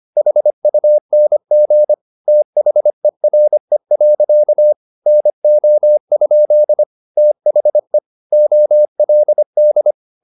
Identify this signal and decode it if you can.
Morse code, heard as a keyed tone, about 25 wpm, 600 Hz. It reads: HUNG THERE. NO? THE OLD